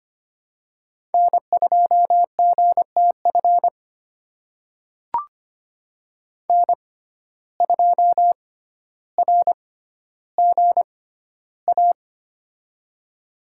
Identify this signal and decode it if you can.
Morse code, heard as a keyed tone, about 25 wpm, spaced as for 13 wpm, 700 Hz. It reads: N2GTF N2RGA
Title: USA Call Signs - 5 Characters - 6x Character Spacing - 25wpm